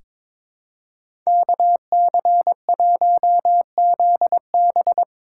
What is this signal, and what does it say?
Morse code, heard as a keyed tone, about 22 wpm, 700 Hz. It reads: KC1ZB
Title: USA Call Signs - 5 Characters - 2x Character Spacing - 22wpm